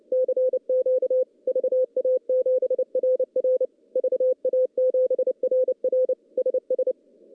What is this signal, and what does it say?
Morse code, heard as a keyed tone, about 29 wpm, 500 Hz. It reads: CQ VA7RR VA7RR SS
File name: contests / CW-SS-k5tr- -0900